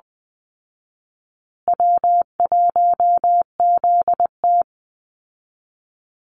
Morse code, heard as a keyed tone, about 20 wpm, 700 Hz. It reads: W1ZT
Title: Call Signs Repeated 2x - 20wpm